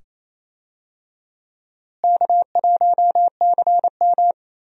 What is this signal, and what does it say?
Morse code, heard as a keyed tone, about 28 wpm, 700 Hz. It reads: K1CM
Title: USA Call Signs - 4 Characters - Repeated 2x - 28wpm